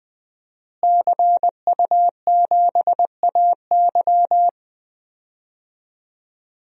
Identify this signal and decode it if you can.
Morse code, heard as a keyed tone, about 21 wpm, 700 Hz. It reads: CU7AY
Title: DXCC-100 Call Signs - 5 Characters - Repeated 3x - 20wpm